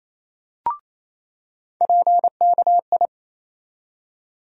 Morse code, heard as a keyed tone, about 28 wpm, 700 Hz. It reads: PKI
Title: USA Call Sign Postfixes - 3 Letters 28wpm